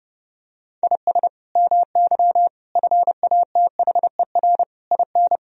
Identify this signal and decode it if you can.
Morse code, heard as a keyed tone, about 30 wpm, 700 Hz. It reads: IS MY FATHER IN
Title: Sentences from Top 1000 Words - Review Part 2 - 30wpm